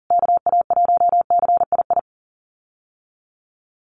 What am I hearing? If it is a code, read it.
Morse code, heard as a keyed tone, about 40 wpm, 700 Hz. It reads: KA1CII
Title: USA Call Signs Repeated 3x 40wpm